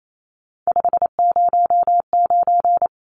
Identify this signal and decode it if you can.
Morse code, heard as a keyed tone, about 28 wpm, 700 Hz. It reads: 509
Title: Numbers - 3-Digit 28wpm